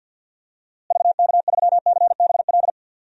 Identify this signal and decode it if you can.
Morse code, heard as a keyed tone, about 47 wpm, 700 Hz. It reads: VK3CBL
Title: DXCC-100 Call Signs - 6 Characters - Repeated 2x - 50wpm